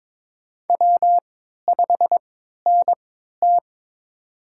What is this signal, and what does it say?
Morse code, heard as a keyed tone, about 22 wpm, spaced as for 13 wpm, 700 Hz. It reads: W5NT
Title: USA Call Signs - 4 Characters - 3x Character Spacing - 22wpm